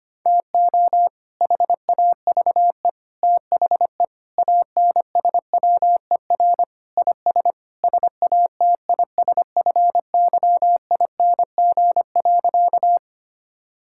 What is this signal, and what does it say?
Morse code, heard as a keyed tone, about 25 wpm, 700 Hz. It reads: TO HAVE THE ANSWER IS SATISFYING.
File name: practice / 25wpm / Sentences with 3-Word Phrases - 25wpm